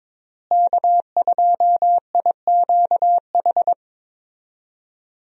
Morse code, heard as a keyed tone, about 22 wpm, 700 Hz. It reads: K2IQH
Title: USA Call Signs - 5 Characters - Repeated 3x - 22wpm